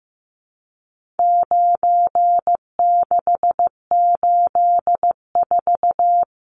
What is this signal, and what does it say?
Morse code, heard as a keyed tone, about 15 wpm, 700 Hz. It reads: 9684
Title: Numbers - 4-Digit 15wpm